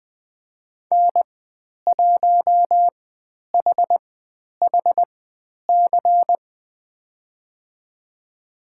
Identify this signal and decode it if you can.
Morse code, heard as a keyed tone, about 20 wpm, spaced as for 10 wpm, 700 Hz. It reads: N1HHC
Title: 109 Lesson 18g - Letter m - Call Signs 20wpm